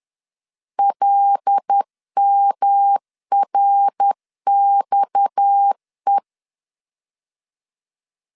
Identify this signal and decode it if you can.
Morse code, heard as a keyed tone, about 10 wpm, 800 Hz. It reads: LMRXE
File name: LEZIONI / 8 YJ45 / 8 YJ 10wpm 2651